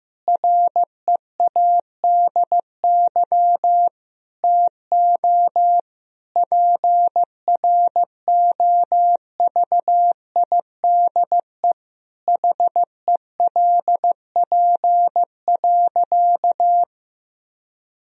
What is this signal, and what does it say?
Morse code, heard as a keyed tone, about 15 wpm, 700 Hz. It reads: READY TO PROVIDE HELP.
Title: Sentences from Top 1000 Words - Review Part 3 - 15wpm